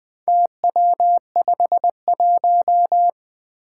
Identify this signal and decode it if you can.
Morse code, heard as a keyed tone, about 20 wpm, 700 Hz. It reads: TW51
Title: Four Challenging Letter-Number Combinations 20wpm